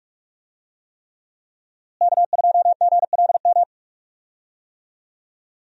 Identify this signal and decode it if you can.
Morse code, heard as a keyed tone, about 45 wpm, 700 Hz. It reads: K2GLM
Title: USA Call Signs - 5 Characters - Repeated 3x - 45wpm